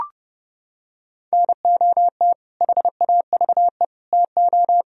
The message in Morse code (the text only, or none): NOT HAVE TO